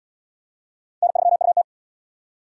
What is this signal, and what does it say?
Morse code, heard as a keyed tone, about 75 wpm, 700 Hz. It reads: N3UT